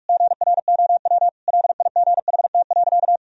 Morse code, heard as a keyed tone, about 46 wpm, 700 Hz. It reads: GROW LIGHT.
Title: Sentences from Top 1000 Words - Review Part 1 - 45wpm